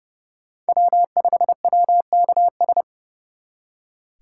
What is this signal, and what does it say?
Morse code, heard as a keyed tone, about 39 wpm, 700 Hz. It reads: W5WKS